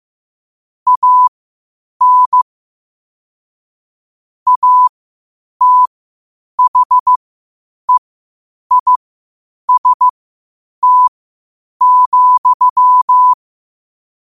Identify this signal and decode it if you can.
Morse code, heard as a keyed tone, about 15 wpm, spaced as for 8 wpm, 1000 Hz. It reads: AN ATHEIST,
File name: 8wpm